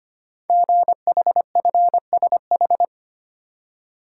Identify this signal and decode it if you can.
Morse code, heard as a keyed tone, about 25 wpm, 700 Hz. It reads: GHFSH
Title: Code Groups - 5 Letters - 25wpm